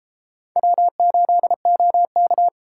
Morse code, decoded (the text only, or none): W8OK